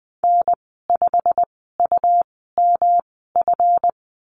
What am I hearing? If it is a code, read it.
Morse code, heard as a keyed tone, about 20 wpm, spaced as for 15 wpm, 700 Hz. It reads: N5UMF
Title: 179 Lesson 29g - Slash - Call Signs 20wpm